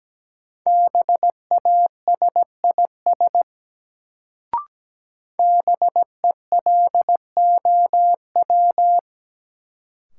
Morse code, heard as a keyed tone, about 17 wpm, 700 Hz. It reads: BASIS BELOW
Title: Sentences from Top 1000 Words - Review Part 5 - New Word Drill - 17wpm